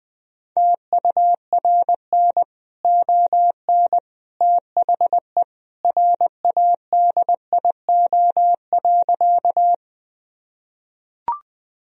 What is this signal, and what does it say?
Morse code, heard as a keyed tone, about 20 wpm, 700 Hz. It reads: TURN ON THE RADIO.
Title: Sentences from Top 1000 Words - Review Part 3 - 20wpm